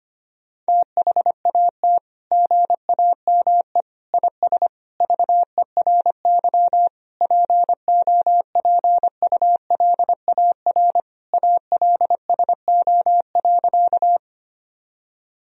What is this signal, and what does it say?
Morse code, heard as a keyed tone, about 25 wpm, 700 Hz. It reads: THAT GAME IS VERY POPULAR ALSO.
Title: Sentences from Top 1000 Words - Review Part 3 - 25wpm